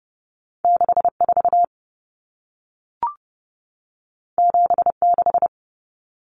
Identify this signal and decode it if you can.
Morse code, heard as a keyed tone, about 30 wpm, 700 Hz. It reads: 64 76
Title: Numbers - 2-Digit 30wpm